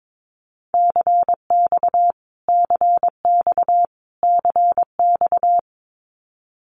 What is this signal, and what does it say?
Morse code, heard as a keyed tone, about 22 wpm, 700 Hz. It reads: CX CX CX
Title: Abbreviations Repeated 3x - 22wpm